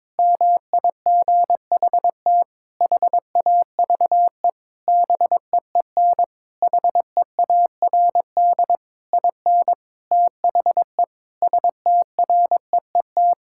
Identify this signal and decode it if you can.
Morse code, heard as a keyed tone, about 22 wpm, 700 Hz. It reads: MIGHT HAVE BEEN HEARD IN THE STREET
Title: Sentences from Top 1000 Words - Review Part 5 - 22wpm